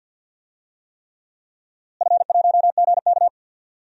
Morse code, heard as a keyed tone, about 52 wpm, 700 Hz. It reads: F1GK
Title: DXCC-100 Call Signs - 4 Characters - Repeated 3x - 50wpm